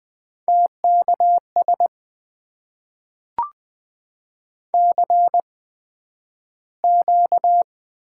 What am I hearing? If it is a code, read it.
Morse code, heard as a keyed tone, about 20 wpm, spaced as for 11 wpm, 700 Hz. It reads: TKS CQ
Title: Top 90 QSO Elements - 8x Character Spacing - 20wpm